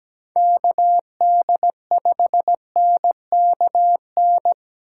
Morse code, heard as a keyed tone, about 17 wpm, 700 Hz. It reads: KD5NKN
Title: USA Call Signs - 2x Character Spacing - 17wpm